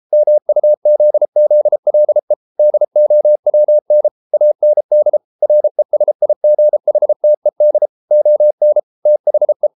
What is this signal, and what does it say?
Morse code, heard as a keyed tone, about 33 wpm, 600 Hz. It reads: MUZZLE DOWN AND RESIGHTED ON THE